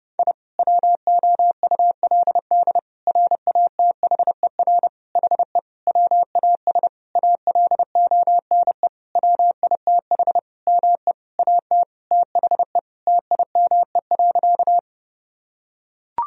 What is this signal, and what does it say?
Morse code, heard as a keyed tone, about 30 wpm, 700 Hz. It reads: I WOULD RATHER HE WAS ALONE WITH ME AT THE TIME.
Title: Sentences from Top 1000 Words - Review Part 4 - 30wpm